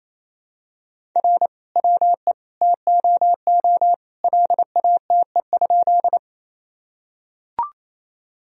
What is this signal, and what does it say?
Morse code, heard as a keyed tone, about 28 wpm, 700 Hz. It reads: R WE TOO LATE?
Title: Sentences from Top 300 Words and Abbreviations 28wpm